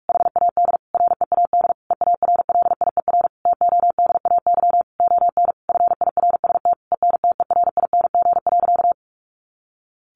Morse code, heard as a keyed tone, about 44 wpm, 700 Hz. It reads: HAD READ EARLIER TODAY ON FIRST ENTERING.
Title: A Sound Of Thunder 45wpm